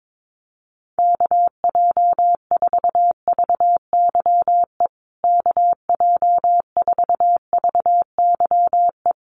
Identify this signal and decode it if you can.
Morse code, heard as a keyed tone, about 22 wpm, 700 Hz. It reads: KJ4VYE KJ4VYE
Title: USA Call Signs Repeated 2x 22wpm